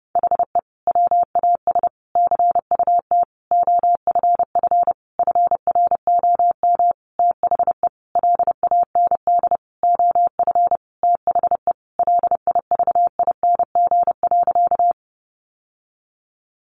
Morse code, heard as a keyed tone, about 30 wpm, 700 Hz. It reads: HE WAS CUT OFF FROM THE LAND OF THE LIVING.
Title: Sentences from Top 1100 Words 30wpm